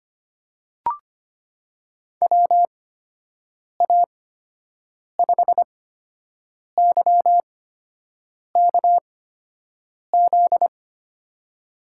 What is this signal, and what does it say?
Morse code, heard as a keyed tone, about 25 wpm, spaced as for 7 wpm, 700 Hz. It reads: WA5YKZ
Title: USA Call Signs - 6 Characters - 8x Character Spacing - 25wpm